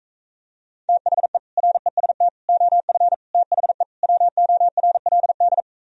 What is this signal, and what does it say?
Morse code, heard as a keyed tone, about 45 wpm, 700 Hz. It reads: THE REST OF THE WORLD